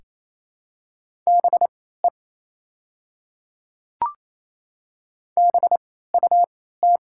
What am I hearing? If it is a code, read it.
Morse code, heard as a keyed tone, about 28 wpm, spaced as for 16 wpm, 700 Hz. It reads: BE BUT
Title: 3x Character Spacing - Top 100 Words - 28wpm